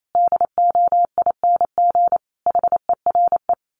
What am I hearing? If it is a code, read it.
Morse code, heard as a keyed tone, about 28 wpm, 700 Hz. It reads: DOING HERE
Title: Sets of 5 Words 28wpm